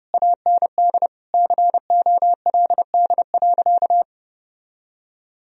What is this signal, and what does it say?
Morse code, heard as a keyed tone, about 30 wpm, 700 Hz. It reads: AND COLD.
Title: Sentences from Top 1000 Words - Review Part 2 - 30wpm